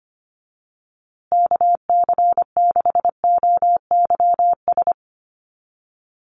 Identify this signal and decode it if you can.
Morse code, heard as a keyed tone, about 25 wpm, 700 Hz. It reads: KC6OYS
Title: USA Call Signs Repeated 2x 25wpm